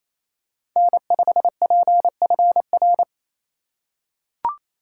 Morse code, heard as a keyed tone, about 27 wpm, 700 Hz. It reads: N5PFR
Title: USA Call Signs - 2x Character Spacing - 28wpm